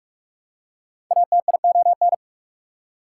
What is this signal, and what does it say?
Morse code, heard as a keyed tone, about 48 wpm, 690 Hz. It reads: ATION
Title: Prefixes and Suffixes 45wpm